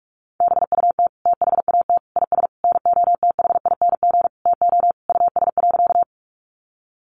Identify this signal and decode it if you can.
Morse code, heard as a keyed tone, about 44 wpm, 700 Hz. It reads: BUT THAT IS NOTHING TO US.
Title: Sentences from Top 500 Words 45wpm